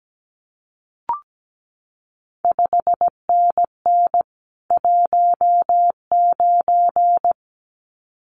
Morse code, heard as a keyed tone, about 17 wpm, 700 Hz. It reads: 5NN 19
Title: CQ WW DX Contest Exchange Encore - 17wpm